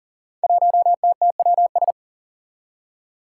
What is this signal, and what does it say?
Morse code, heard as a keyed tone, about 40 wpm, 700 Hz. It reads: 1TTWS
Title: Five Challenging Letter-Number Combinations 40wpm